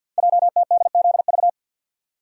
Morse code, heard as a keyed tone, about 51 wpm, 700 Hz. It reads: JTDZV
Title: Code Groups - 5 Letters - 50wpm